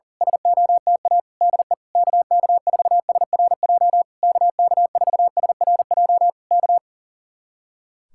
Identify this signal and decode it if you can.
Morse code, heard as a keyed tone, about 37 wpm, 700 Hz. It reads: SOTA DE KK4SRJ KK4SRJ K
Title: Speed-Racing Common QSO Phrases - 40wpm